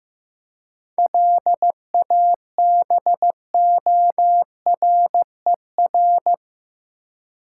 Morse code, heard as a keyed tone, about 16 wpm, 700 Hz. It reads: LABORER
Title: Practice QSO - Exchange 3 - More Info - 15wpm